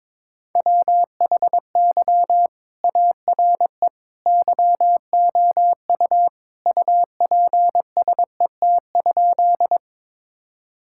Morse code, heard as a keyed tone, about 22 wpm, 700 Hz. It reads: WHY ARE YOU UPSET?
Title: Sentences from Top 1000 Words - Review Part 5 - 22wpm